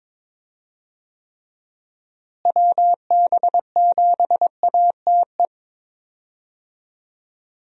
Morse code, heard as a keyed tone, about 21 wpm, 700 Hz. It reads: WB7ATE
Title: USA Call Signs - 6 Characters - Repeated 2x - 22wpm